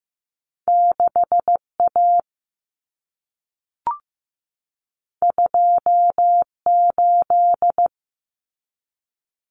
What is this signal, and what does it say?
Morse code, heard as a keyed tone, about 15 wpm, 700 Hz. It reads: 6A 28
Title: Common Cut Numbers - 2-Digit 15wpm